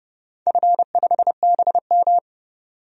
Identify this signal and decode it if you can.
Morse code, heard as a keyed tone, about 30 wpm, 700 Hz. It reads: F5BM
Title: DXCC-100 Call Signs - 4 Characters - 30wpm